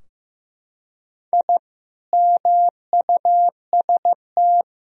Morse code, heard as a keyed tone, about 15 wpm, 700 Hz. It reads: I MUST
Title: Sentences from Top 1000 Words - Review Part 4 - 15wpm